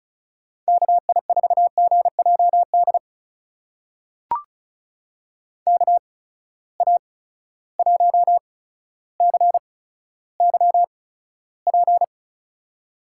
Morse code, heard as a keyed tone, about 36 wpm, spaced as for 15 wpm, 700 Hz. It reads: KI4GJD KA1CYP
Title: USA Call Signs - 6 Characters - 8x Character Spacing - 35wpm